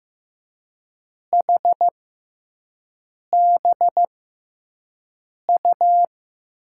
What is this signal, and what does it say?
Morse code, heard as a keyed tone, about 15 wpm, spaced as for 5 wpm, 700 Hz. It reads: HBU
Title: USA Call Signs - 6x Character Spacing - 15wpm